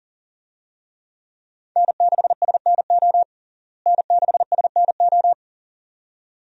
Morse code, heard as a keyed tone, about 40 wpm, 700 Hz. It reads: N6SNO N6SNO
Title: USA Call Signs - 5 Characters - Repeated 2x - 40wpm